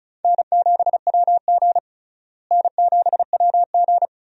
Morse code, heard as a keyed tone, about 34 wpm, 700 Hz. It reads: N7WG N7WG
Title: USA Call Signs - 4 Characters - Repeated 2x - 35wpm